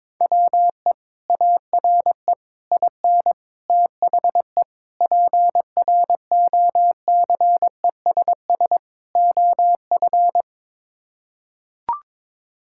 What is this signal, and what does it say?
Morse code, heard as a keyed tone, about 22 wpm, 700 Hz. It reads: WE ARE IN THE PROCESS OF
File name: practice / 22wpm / Sets of 6 Words - 22wpm